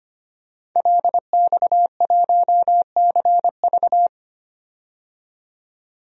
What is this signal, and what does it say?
Morse code, heard as a keyed tone, about 25 wpm, 700 Hz. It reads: LX1CV
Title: DXCC-100 Call Signs - 5 Characters - 25wpm